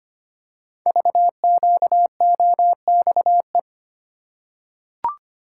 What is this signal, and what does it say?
Morse code, heard as a keyed tone, about 25 wpm, 700 Hz. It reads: VQOXE
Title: Code Groups - 5 Letters - 25wpm